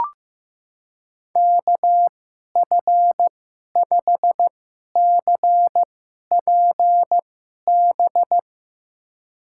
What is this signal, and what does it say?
Morse code, heard as a keyed tone, about 15 wpm, spaced as for 11 wpm, 700 Hz. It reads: KF5CPB
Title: USA Call Signs - 6 Characters - 2x Character Spacing - 15wpm